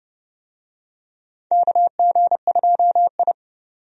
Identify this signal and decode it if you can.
Morse code, heard as a keyed tone, about 29 wpm, 700 Hz. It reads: KG2I
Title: USA Call Signs - 4 Characters - 30wpm